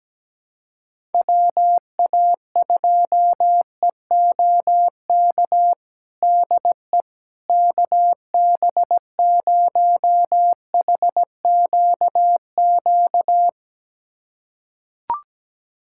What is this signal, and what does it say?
Morse code, heard as a keyed tone, about 17 wpm, 700 Hz. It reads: WA2EOK DE KB0HQQ
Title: USA Callsign DE Callsign - 6 Characters - 17wpm